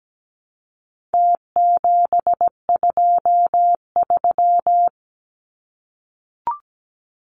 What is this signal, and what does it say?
Morse code, heard as a keyed tone, about 17 wpm, 700 Hz. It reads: T723
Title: Numbers - Contest Style - 4-Digit 17wpm